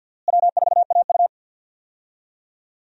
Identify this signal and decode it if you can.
Morse code, heard as a keyed tone, about 51 wpm, 700 Hz. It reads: W4AU
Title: USA Call Signs - 4 Characters - Repeated 2x - 50wpm